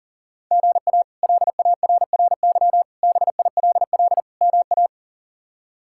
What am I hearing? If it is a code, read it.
Morse code, heard as a keyed tone, about 40 wpm, 700 Hz. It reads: GA LARRY BILL MA
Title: K1USN SST QSO Practice - 40wpm